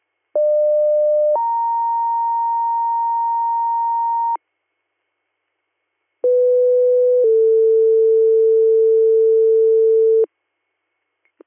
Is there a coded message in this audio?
Yes: two-tone page from 0.3 to 10.2 s, heard as a one-second tone, then a three-second tone, about 600 then 930 Hz, about 500 then 450 Hz.